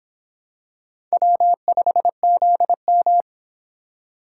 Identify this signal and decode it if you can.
Morse code, heard as a keyed tone, about 25 wpm, 700 Hz. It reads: W5ZM